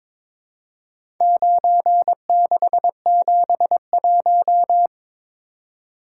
Morse code, heard as a keyed tone, about 22 wpm, 700 Hz. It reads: 9671
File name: practice / 22wpm / Four Letter-Number Combinations - 22wpm